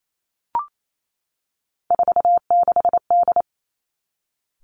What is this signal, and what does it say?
Morse code, heard as a keyed tone, about 28 wpm, 700 Hz. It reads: 46D
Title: Three Challenging Letter-Number Combinations 28wpm